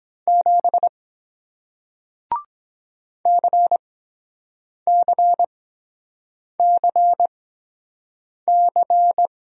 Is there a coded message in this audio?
Yes: Morse code from 0.3 to 9.3 s, heard as a keyed tone, about 23 wpm, spaced as for 7 wpm, 700 Hz. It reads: 7 CCCC